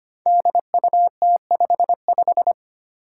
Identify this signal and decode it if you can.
Morse code, heard as a keyed tone, about 25 wpm, 700 Hz. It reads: DUT55
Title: Five Challenging Letter-Number Combinations 25wpm